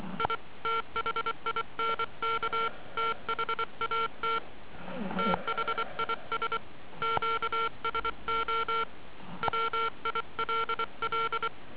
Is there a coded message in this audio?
Yes: Morse code from 0.2 to 11.5 s, heard as a keyed tone, about 24 wpm, 450 Hz. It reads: I THINK THAT THIS QSO WILL